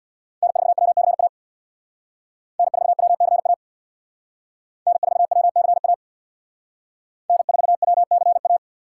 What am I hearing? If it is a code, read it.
Morse code, heard as a keyed tone, about 62 wpm, 700 Hz. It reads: N4WCA N4WCA N4WCA N4WCA
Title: Speed Racing - USA 1x3 Call Signs 50wpm